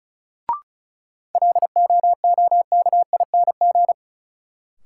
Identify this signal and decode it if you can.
Morse code, heard as a keyed tone, about 34 wpm, 700 Hz. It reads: LOOKING
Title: Sentences from Top 1000 Words - Review Part 6 - New Word Drill - 35wpm